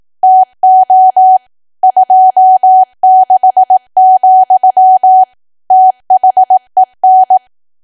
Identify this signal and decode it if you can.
Morse code, heard as a keyed tone, about 18 wpm, 750 Hz. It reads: TO 26, THEN
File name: cwa / W1AW-Level2 / 18WPM